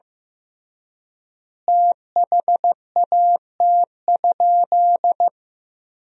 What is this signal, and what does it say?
Morse code, heard as a keyed tone, about 15 wpm, 700 Hz. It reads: THAT?
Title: 3x Word-Spacing - Sentences from Top 300 Words - 15wpm